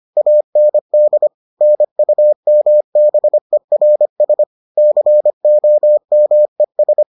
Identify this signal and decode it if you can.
Morse code, heard as a keyed tone, about 25 wpm, 600 Hz. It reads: AND NUMBERS COMES